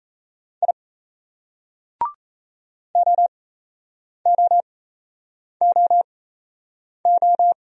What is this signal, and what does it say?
Morse code, heard as a keyed tone, about 35 wpm, spaced as for 8 wpm, 700 Hz. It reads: I OOOO
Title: Speed-Racing Single Letters 28wpm